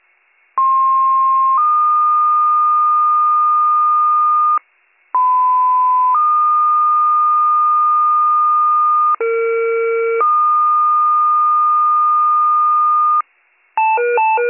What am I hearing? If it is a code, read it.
Two-tone page, heard as a one-second tone, then a three-second tone, about 1060 then 1220 Hz, about 1010 then 1220 Hz, about 460 then 1220 Hz. It is heard from 0.6 to 13.2 s.